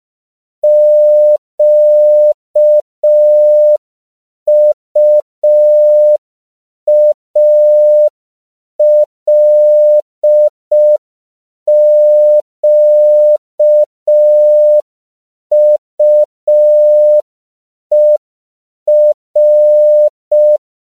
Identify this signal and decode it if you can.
Morse code, heard as a keyed tone, about 5 wpm, 600 Hz. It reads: QUALQUER
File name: CW / 5PPM